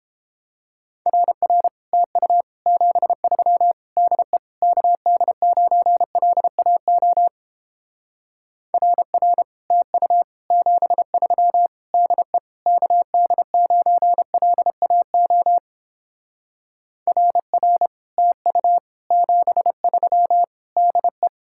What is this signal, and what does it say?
Morse code, heard as a keyed tone, about 29 wpm, 700 Hz. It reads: RR TU 73 DE KD9LAO RR TU 73 DE KD9LAO RR TU 73 DE